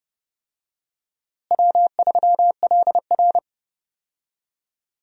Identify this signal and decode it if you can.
Morse code, heard as a keyed tone, about 31 wpm, 700 Hz. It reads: W3LR
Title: USA Call Signs - 4 Characters - Repeated 3x - 30wpm